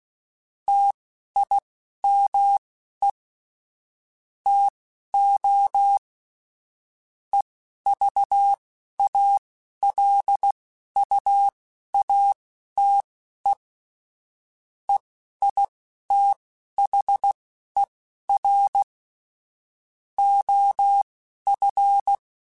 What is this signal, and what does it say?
Morse code, heard as a keyed tone, about 16 wpm, spaced as for 11 wpm, 800 Hz. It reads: TIME TO EVALUATE EITHER OF
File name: Technical Subjects / Technical Subjects-mp3s / AND Gate @10